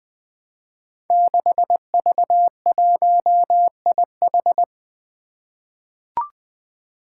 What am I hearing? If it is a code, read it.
Morse code, heard as a keyed tone, about 20 wpm, 700 Hz. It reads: 6V1IH
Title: Five Challenging Letter-Number Combinations 20wpm